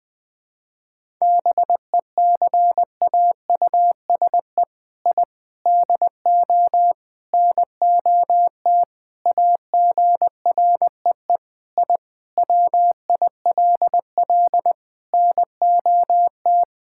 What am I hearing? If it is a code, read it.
Morse code, heard as a keyed tone, about 20 wpm, 700 Hz. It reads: BECAUSE I DO NOT AGREE I WILL NOT
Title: Sentences with 4-Word Phrases - 20wpm